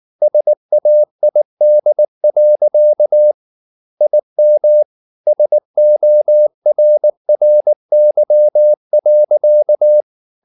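Morse code, heard as a keyed tone, about 19 wpm, 600 Hz. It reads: SAID. IM SORRY.
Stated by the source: Morse code at 19 wpm from CW med Gnister, reading Ken Follett's novel Never in English.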